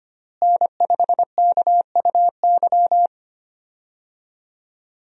Morse code, heard as a keyed tone, about 25 wpm, 700 Hz. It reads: N5KUY
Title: USA Call Signs - 5 Characters - Repeated 3x - 25wpm